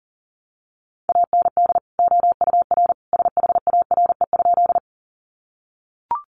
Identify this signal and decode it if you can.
Morse code, heard as a keyed tone, about 40 wpm, 700 Hz. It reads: AND OUR SHARE?
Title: Sentences from Top 900 Words 40wpm